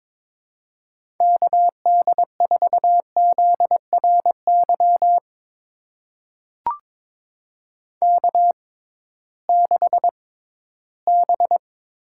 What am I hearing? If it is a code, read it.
Morse code, heard as a keyed tone, about 22 wpm, spaced as for 14 wpm, 700 Hz. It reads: KD4ZRY K6B
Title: USA Call Signs - 6x Character Spacing - 22wpm